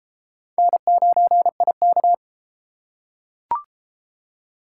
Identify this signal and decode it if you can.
Morse code, heard as a keyed tone, about 30 wpm, 700 Hz. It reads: N9IK